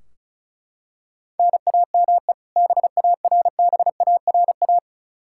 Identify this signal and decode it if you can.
Morse code, heard as a keyed tone, about 36 wpm, 700 Hz. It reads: NAME BARBARA
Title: Practice QSO - Exchange 1 - Beginning - 35wpm